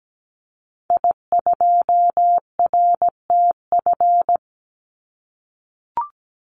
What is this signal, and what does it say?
Morse code, heard as a keyed tone, about 17 wpm, 700 Hz. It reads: I2RTF
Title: Call Signs - 17wpm